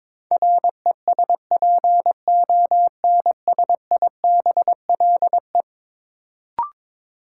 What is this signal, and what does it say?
Morse code, heard as a keyed tone, about 22 wpm, 700 Hz. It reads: RESPONSIBLE
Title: Sentences from Top 1000 Words - Review Part 7 - New Word Drill - 22wpm